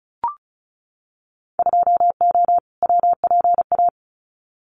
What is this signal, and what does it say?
Morse code, heard as a keyed tone, about 34 wpm, 700 Hz. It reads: 2O WPA
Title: Winter Field Day Exchanges 35wpm